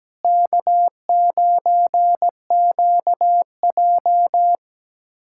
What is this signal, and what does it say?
Morse code, heard as a keyed tone, about 17 wpm, 700 Hz. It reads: K9QJ
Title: USA Callsign DE Callsign - 4 Characters - 17wpm